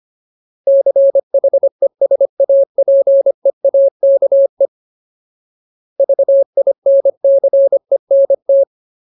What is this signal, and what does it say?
Morse code, heard as a keyed tone, about 25 wpm, 550 Hz. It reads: CHESAPEAKE VINCENT
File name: Ultimate Word List / 25WPM Ultimate